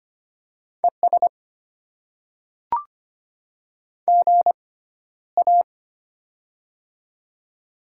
Morse code, heard as a keyed tone, about 25 wpm, spaced as for 13 wpm, 700 Hz. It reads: ES GA
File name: Top 20 QSO Elements - 6x Character Spacing - 25wpm